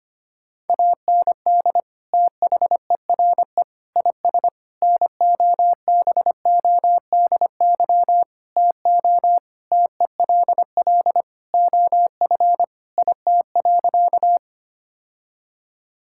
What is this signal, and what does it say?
Morse code, heard as a keyed tone, about 25 wpm, 700 Hz. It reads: AND THERE IS NOBODY TO TELL OF IT.
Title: Sentences from Top 1600 Words 25wpm